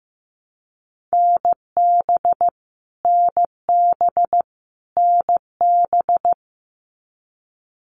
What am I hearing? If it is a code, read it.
Morse code, heard as a keyed tone, about 15 wpm, 700 Hz. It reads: NB NB NB
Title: Canadian Provinces and Territories - Repeated 3x - 15wpm